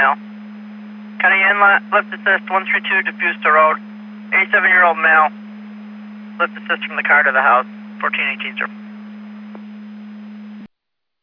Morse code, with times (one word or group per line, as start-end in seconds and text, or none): none